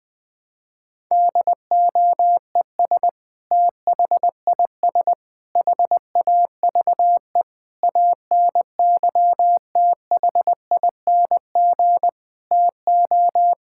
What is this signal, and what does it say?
Morse code, heard as a keyed tone, about 21 wpm, 700 Hz. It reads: DOES THIS HAVE ANYTHING TO
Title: Sentences with 5-Word Phrases - 20wpm